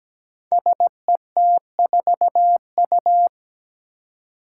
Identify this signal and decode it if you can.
Morse code, heard as a keyed tone, about 17 wpm, 700 Hz. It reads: SET4U
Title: Five Challenging Letter-Number Combinations 17wpm